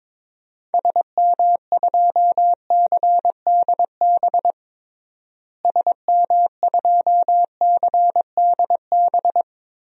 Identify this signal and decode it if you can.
Morse code, heard as a keyed tone, about 22 wpm, 700 Hz. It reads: SM2CDB SM2CDB
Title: DXCC-100 Call Signs - 6 Characters - Repeated 2x - 22wpm